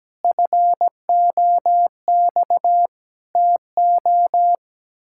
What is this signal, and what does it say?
Morse code, heard as a keyed tone, about 17 wpm, 700 Hz. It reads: FOX TO
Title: Pangram Challenge v2 - 17wpm